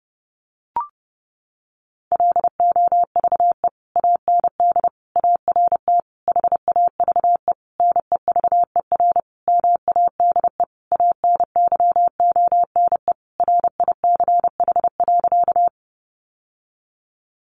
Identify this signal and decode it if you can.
Morse code, heard as a keyed tone, about 30 wpm, 700 Hz. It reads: LOVE AND ART HAVE NEVER MADE ANYONE RICH.
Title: Sentences from Top 1100 Words 30wpm